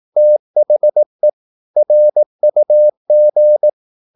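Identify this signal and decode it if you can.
Morse code, heard as a keyed tone, about 18 wpm, 600 Hz. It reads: THE RUG
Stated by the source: Morse practice at 18 wpm, English text, from CW med Gnister.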